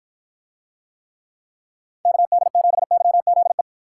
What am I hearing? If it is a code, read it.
Morse code, heard as a keyed tone, about 53 wpm, 700 Hz. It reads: KD7YZE